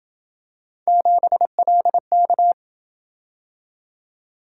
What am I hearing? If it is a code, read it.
Morse code, heard as a keyed tone, about 27 wpm, 700 Hz. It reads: 7LK